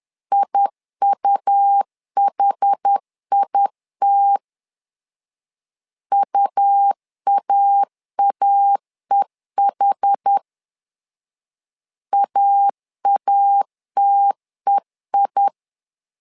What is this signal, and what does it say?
Morse code, heard as a keyed tone, about 10 wpm, 800 Hz. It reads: IUHIT UAAEH AATEI
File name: LEZIONI / 2 UHDN / 2 UH 10wpm 1714